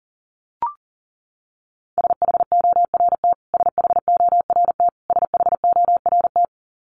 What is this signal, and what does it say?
Morse code, heard as a keyed tone, about 37 wpm, 700 Hz. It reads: SHORT SHORT SHORT
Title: Top 500 Words Repeated 40wpm